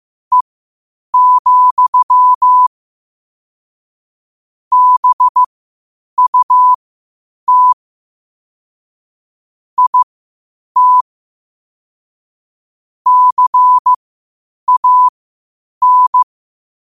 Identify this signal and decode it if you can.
Morse code, heard as a keyed tone, about 15 wpm, spaced as for 8 wpm, 1000 Hz. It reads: E, BUT IT CAN
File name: qotdfiles / 8wpm